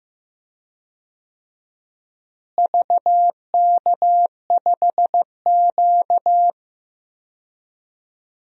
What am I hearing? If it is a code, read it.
Morse code, heard as a keyed tone, about 15 wpm, 700 Hz. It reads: VK5Q
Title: DXCC-100 Call Signs - 4 Characters - Repeated 2x - 15wpm